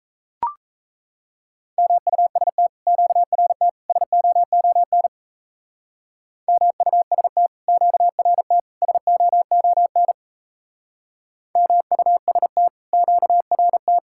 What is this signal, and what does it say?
Morse code, heard as a keyed tone, about 38 wpm, 700 Hz. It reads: MUST QRT SOON MUST QRT SOON MUST QRT